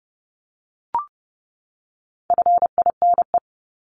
Morse code, heard as a keyed tone, about 30 wpm, 700 Hz. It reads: FINE
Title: Top 500 Words 30wpm